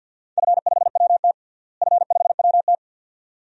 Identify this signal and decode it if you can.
Morse code, heard as a keyed tone, about 52 wpm, 700 Hz. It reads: F5WT F5WT
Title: DXCC-100 Call Signs - 4 Characters - Repeated 2x - 50wpm